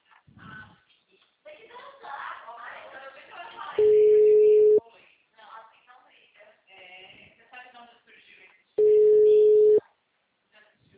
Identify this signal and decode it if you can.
Ringback tone (Europe, 425 Hz), with speech.
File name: out- -2006- - - .2